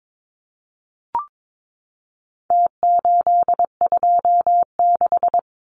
Morse code, heard as a keyed tone, about 22 wpm, 700 Hz. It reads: T826
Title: Numbers - Contest Style - 4-Digit 22wpm